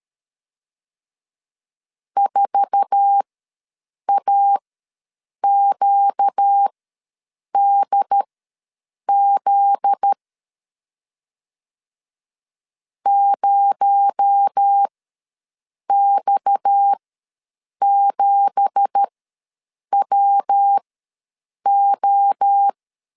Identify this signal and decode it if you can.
Morse code, heard as a keyed tone, about 13 wpm, spaced as for 7 wpm, 800 Hz. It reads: 4AQDZ 0X7WO
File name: LEZIONI / 10 SIGNS / 10B INT ws 12wpm 3527